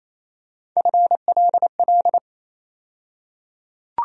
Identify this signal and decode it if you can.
Morse code, heard as a keyed tone, about 28 wpm, 700 Hz. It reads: FLL
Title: USA Call Sign Postfixes - 3 Letters 28wpm